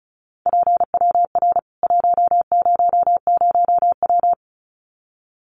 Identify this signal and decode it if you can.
Morse code, heard as a keyed tone, about 34 wpm, 700 Hz. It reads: PWR 100W
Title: Common QSO Phrases 35wpm